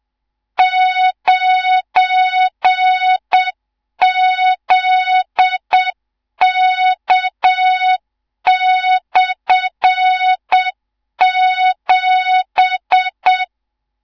Morse code, heard as a keyed tone, about 7 wpm, 3700 Hz, with noise. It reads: 9ZK/7
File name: Morze-HA2MN / hang / 18 IRASJELEK2